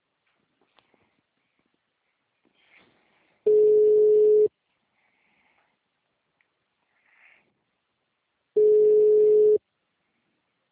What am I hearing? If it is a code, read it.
Ringback tone (Europe, 425 Hz).